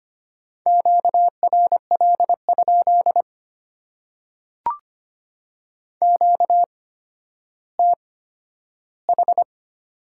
Morse code, heard as a keyed tone, about 25 wpm, spaced as for 12 wpm, 690 Hz. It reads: QRL? QTH
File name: Top 90 QSO Elements - 8x Character Spacing - 25wpm